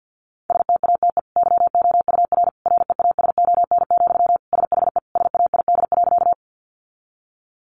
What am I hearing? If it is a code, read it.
Morse code, heard as a keyed tone, about 50 wpm, 700 Hz. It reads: STATE YOUR REASON, SHE SAID.
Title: Sentences from Top 700 Words 50wpm